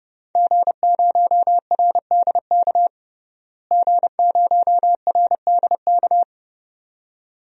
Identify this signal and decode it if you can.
Morse code, heard as a keyed tone, about 30 wpm, 700 Hz. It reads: G0RDK G0RDK
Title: DXCC-100 Call Signs - 5 Characters - Repeated 2x - 30wpm